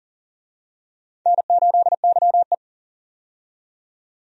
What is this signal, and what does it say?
Morse code, heard as a keyed tone, about 40 wpm, 700 Hz. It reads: N8YE
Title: USA Call Signs - 4 Characters - 40wpm